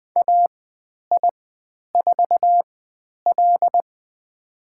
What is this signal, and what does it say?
Morse code, heard as a keyed tone, about 20 wpm, spaced as for 10 wpm, 700 Hz. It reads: AI4L
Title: 095 Lesson 16g - Letter c - Call Signs 20wpm